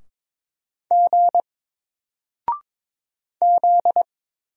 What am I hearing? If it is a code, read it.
Morse code, heard as a keyed tone, about 22 wpm, 700 Hz. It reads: G Z
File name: Single-Letter-Number-22wpm